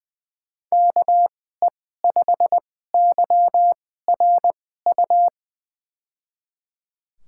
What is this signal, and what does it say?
Morse code, heard as a keyed tone, about 21 wpm, spaced as for 15 wpm, 700 Hz. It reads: KE5YRU
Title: USA Call Signs - 6 Characters - 2x Character Spacing - 20wpm